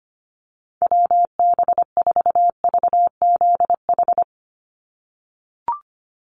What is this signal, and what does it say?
Morse code, heard as a keyed tone, about 25 wpm, 700 Hz. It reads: WB4VZH
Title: USA Call Signs 25wpm